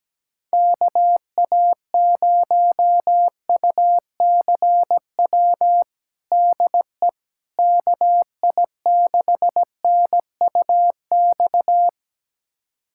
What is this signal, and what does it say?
Morse code, heard as a keyed tone, about 17 wpm, 700 Hz. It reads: KA0UCW DE KI6NUX
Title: USA Callsign DE Callsign - 6 Characters - 17wpm